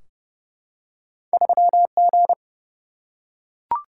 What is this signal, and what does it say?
Morse code, heard as a keyed tone, about 30 wpm, 700 Hz. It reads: 3G
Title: Single-Letter Single-Number 30wpm